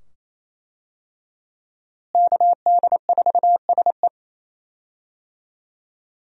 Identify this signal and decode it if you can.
Morse code, heard as a keyed tone, about 28 wpm, 700 Hz. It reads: KD4SE
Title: USA Call Signs - 5 Characters - Repeated 2x - 28wpm